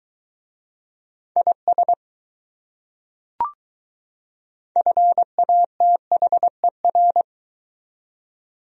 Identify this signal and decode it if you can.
Morse code, heard as a keyed tone, about 20 wpm, 700 Hz. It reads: IS FATHER